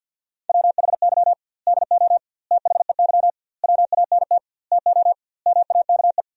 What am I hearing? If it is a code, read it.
Morse code, heard as a keyed tone, about 50 wpm, 700 Hz. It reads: WHY DO THEY WANT TO MAKE
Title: 2x Word-Spacing - Sentences from Top 400 Words - 50wpm